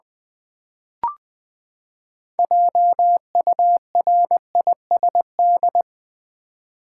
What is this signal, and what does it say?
Morse code, heard as a keyed tone, about 20 wpm, 700 Hz. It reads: JURISD
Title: Build A Word - 6 Letters At A Time - 12-Letter Words - 20wpm